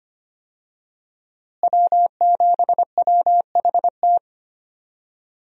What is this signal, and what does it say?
Morse code, heard as a keyed tone, about 25 wpm, 700 Hz. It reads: W7WHT
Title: USA Call Signs - 5 Characters - Repeated 3x - 25wpm